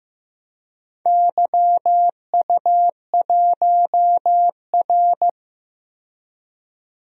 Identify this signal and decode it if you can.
Morse code, heard as a keyed tone, about 15 wpm, 700 Hz. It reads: YU1R
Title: DXCC-100 Call Signs - 4 Characters - 15wpm